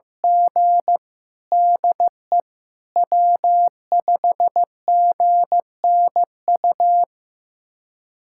Morse code, heard as a keyed tone, about 16 wpm, 700 Hz. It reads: G DE W5GNU
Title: USA Callsign DE Callsign - 5 Characters - 15wpm